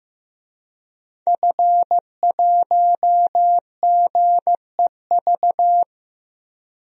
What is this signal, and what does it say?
Morse code, heard as a keyed tone, about 15 wpm, 700 Hz. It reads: F1GEV
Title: DXCC-100 Call Signs - 5 Characters - Repeated 3x - 15wpm